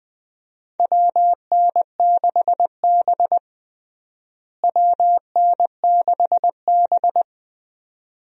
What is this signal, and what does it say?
Morse code, heard as a keyed tone, about 20 wpm, 700 Hz. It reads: WN6B WN6B
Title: USA Call Signs - 4 Characters - Repeated 2x - 20wpm